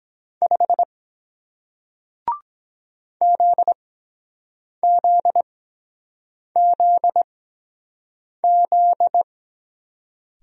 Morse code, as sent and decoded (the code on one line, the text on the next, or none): ..... / --.. --.. --.. --..
5 ZZZZ